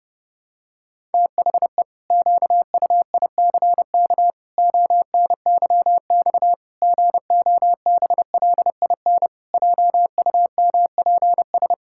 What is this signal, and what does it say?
Morse code, heard as a keyed tone, about 30 wpm, 700 Hz. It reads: THE QUICK ONYX GOBLIN JUMPS
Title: Pangram Challenge v2 - 30wpm